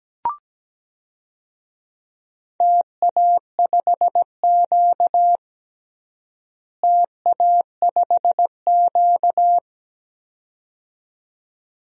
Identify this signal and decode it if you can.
Morse code, heard as a keyed tone, about 17 wpm, 700 Hz. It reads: TA5Q TA5Q
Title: DXCC-100 Call Signs - 4 Characters - Repeated 2x - 17wpm